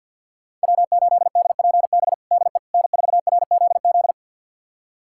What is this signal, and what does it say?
Morse code, heard as a keyed tone, about 52 wpm, 700 Hz. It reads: W8DPB DE N4RZZ
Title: USA Callsign DE Callsign - 5 Characters - 50wpm